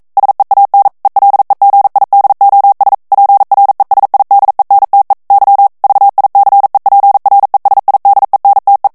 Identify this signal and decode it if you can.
Morse code, heard as a keyed tone, about 42 wpm, 800 Hz. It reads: SEAN ELEGIDOS PRESIDENTE Y VICEPRESIDENTE
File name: morse mp3 / ESPAGNOL TEXTE EN CLAIR 1 VITESSE 40 MOTS MINUTES